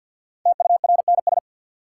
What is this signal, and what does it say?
Morse code, heard as a keyed tone, about 52 wpm, 700 Hz. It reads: TURNS
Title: Sets of 5 Words Encore - New Word Drill - 50wpm